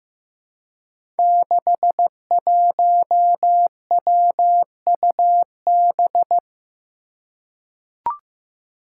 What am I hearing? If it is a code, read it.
Morse code, heard as a keyed tone, about 15 wpm, 700 Hz. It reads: 61WUB
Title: Five Challenging Letter-Number Combinations 15wpm